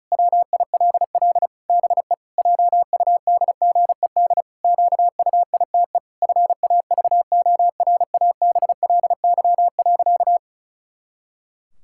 Morse code, heard as a keyed tone, about 36 wpm, 700 Hz. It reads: WILL BE JUDGED QUITE FAVORABLY.
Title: Pangram Challenge v2 - 35wpm